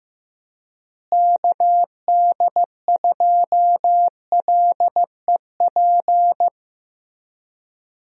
Morse code, heard as a keyed tone, about 15 wpm, 700 Hz. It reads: KD2LEP
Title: USA Call Signs - 6 Characters - Repeated 3x - 15wpm